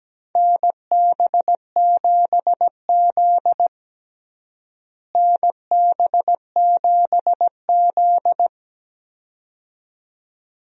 Morse code, heard as a keyed tone, about 17 wpm, 700 Hz. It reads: NB7Z NB7Z
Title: USA Call Signs - 4 Characters - Repeated 2x - 17wpm